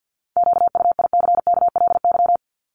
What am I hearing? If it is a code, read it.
Morse code, heard as a keyed tone, about 50 wpm, 700 Hz. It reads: QUICKLY
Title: Top 501-750 Words 50wpm